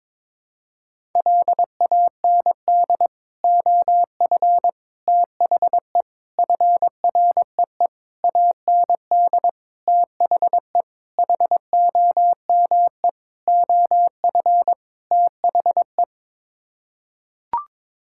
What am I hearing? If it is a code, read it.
Morse code, heard as a keyed tone, about 22 wpm, 700 Hz. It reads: LAND OF THE FREE AND THE HOME OF THE
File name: Sets of 9 Words - 22wpm